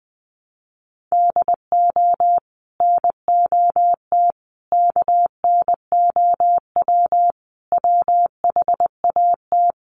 Morse code, heard as a keyed tone, about 20 wpm, 700 Hz. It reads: DO NOT KNOW WHAT
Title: Sets of 5 Words 20wpm